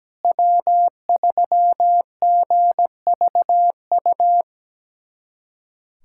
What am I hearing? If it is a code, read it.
Morse code, heard as a keyed tone, about 17 wpm, 700 Hz. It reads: W3GVU
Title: USA Callsign DE Callsign - 5 Characters - 17wpm